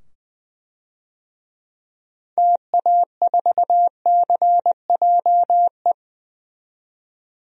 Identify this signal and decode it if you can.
Morse code, heard as a keyed tone, about 20 wpm, 700 Hz. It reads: TA4CJE
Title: DXCC-100 Call Signs - 6 Characters - 20wpm